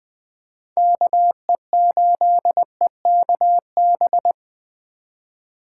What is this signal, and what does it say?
Morse code, heard as a keyed tone, about 21 wpm, 700 Hz. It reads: KE8EKB